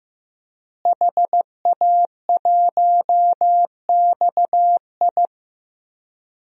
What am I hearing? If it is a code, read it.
Morse code, heard as a keyed tone, about 15 wpm, 700 Hz. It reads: HA1XI
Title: DXCC-100 Call Signs - 5 Characters - Repeated 3x - 15wpm